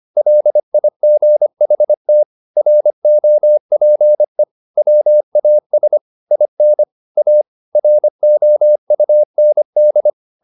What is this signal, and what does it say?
Morse code, heard as a keyed tone, about 25 wpm, 600 Hz. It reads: LIGHT ROPE WAS IN A ROUND